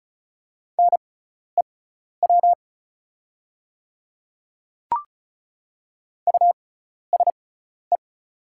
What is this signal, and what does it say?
Morse code, heard as a keyed tone, about 36 wpm, spaced as for 12 wpm, 690 Hz. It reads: NEW USE
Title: 6x Character Spacing - Top 300 Words - 35wpm